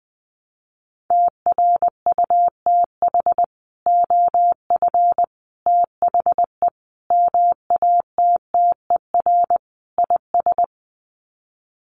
Morse code, heard as a keyed tone, about 20 wpm, 700 Hz. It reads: TRUTH OF THE MATTER IS
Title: Sets of 5 Words Encore2 20wpm